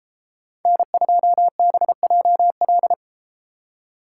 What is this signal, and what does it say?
Morse code, heard as a keyed tone, about 33 wpm, 690 Hz. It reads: N2BJL